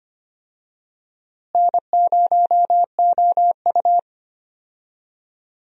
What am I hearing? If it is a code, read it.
Morse code, heard as a keyed tone, about 25 wpm, 700 Hz. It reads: N0OU
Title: USA Call Signs - 4 Characters - Repeated 3x - 25wpm